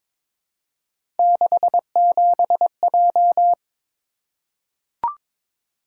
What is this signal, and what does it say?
Morse code, heard as a keyed tone, about 23 wpm, 700 Hz. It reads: 67J
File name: practice / 22wpm / Three Letter-Number Combinations - 22wpm